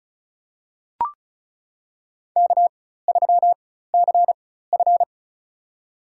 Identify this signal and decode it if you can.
Morse code, heard as a keyed tone, about 34 wpm, spaced as for 16 wpm, 700 Hz. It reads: K3CF
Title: USA Call Signs - 4 Characters - 4x Character Spacing - 35wpm